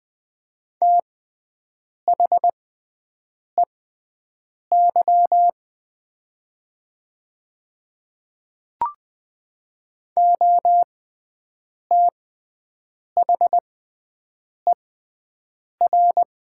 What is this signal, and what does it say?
Morse code, heard as a keyed tone, about 20 wpm, spaced as for 7 wpm, 700 Hz. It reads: THEY OTHER
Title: 6x Character Spacing - Top 200 Words - 20wpm